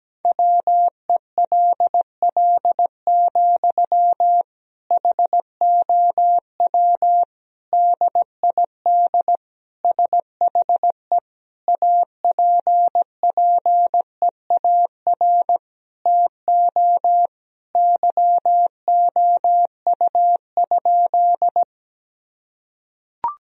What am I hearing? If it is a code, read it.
Morse code, heard as a keyed tone, about 17 wpm, 700 Hz. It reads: WELL, HOW DID SHE APPEAR TO YOU?
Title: Sentences from Top 1000 Words - Review Part 2 - 17wpm